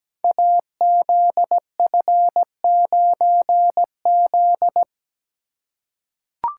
Morse code, heard as a keyed tone, about 17 wpm, 700 Hz. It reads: AZF9Z
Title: Code Groups - 5 Letter:Numbers - 17wpm